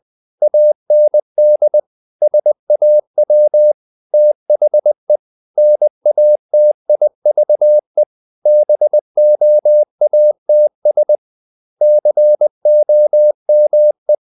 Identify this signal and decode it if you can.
Morse code, heard as a keyed tone, about 20 wpm, 600 Hz. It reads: AND SAW THE NATIVE BOATS COME
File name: the old man and the sea / the old man and the sea part 1 20wpm